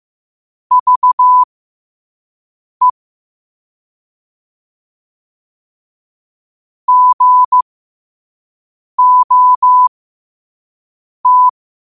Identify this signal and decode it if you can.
Morse code, heard as a keyed tone, about 15 wpm, spaced as for 5 wpm, 1000 Hz. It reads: VE GOT
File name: qotdfiles / 5wpm